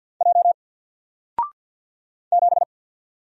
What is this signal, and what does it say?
Morse code, heard as a keyed tone, about 48 wpm, 700 Hz. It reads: J 7
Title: Single Character (Letter:Number:Punctuation:Prosign) Alt 50wpm